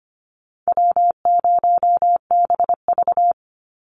Morse code, heard as a keyed tone, about 25 wpm, 700 Hz. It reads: W0BV
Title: Easy Call Signs Repeated 3x - 25wpm